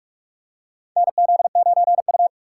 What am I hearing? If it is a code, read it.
Morse code, heard as a keyed tone, about 42 wpm, 700 Hz. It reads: NZ9U
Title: USA Call Signs - 4 Characters - 6x Character Spacing - 45wpm